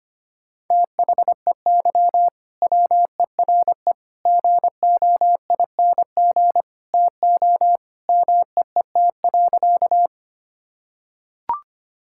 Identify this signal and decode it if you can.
Morse code, heard as a keyed tone, about 25 wpm, 700 Hz. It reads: THEY WERE GOING TO MEET.
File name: practice / 25wpm / Sentences with 3-Word Phrases - 25wpm